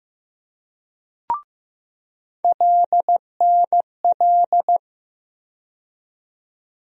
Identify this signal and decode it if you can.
Morse code, heard as a keyed tone, about 15 wpm, 700 Hz. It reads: LNL
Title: USA Call Sign Postfixes - 3 Letters 15wpm